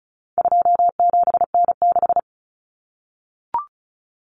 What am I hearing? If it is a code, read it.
Morse code, heard as a keyed tone, about 34 wpm, 700 Hz. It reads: 27N6
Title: Common Cut Numbers - 4-Digit 35wpm